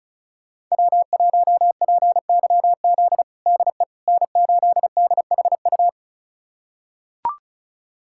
Morse code, heard as a keyed tone, about 36 wpm, 700 Hz. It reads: W1PYZ DE N8DHU
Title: USA Callsign DE Callsign - 5 Characters - 35wpm